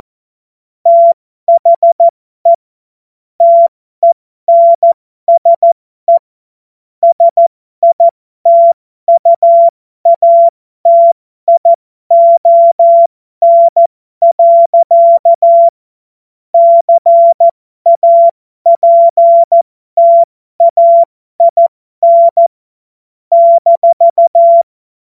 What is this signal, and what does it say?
Morse code, heard as a keyed tone, about 14 wpm, spaced as for 12 wpm, 680 Hz. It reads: THE TENSE SITUATION. CAPTAIN -
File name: Morsekurs / Witze